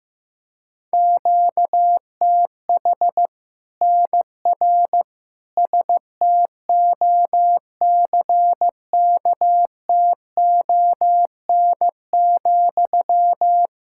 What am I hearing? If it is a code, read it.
Morse code, heard as a keyed tone, about 16 wpm, 700 Hz. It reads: QTH NR STOCKTON,
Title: Practice QSO - Exchange 1 - Beginning - 15wpm